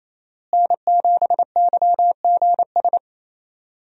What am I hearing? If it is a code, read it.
Morse code, heard as a keyed tone, about 28 wpm, 700 Hz. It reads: N7YGS
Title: USA Call Signs - 5 Characters - Repeated 2x - 28wpm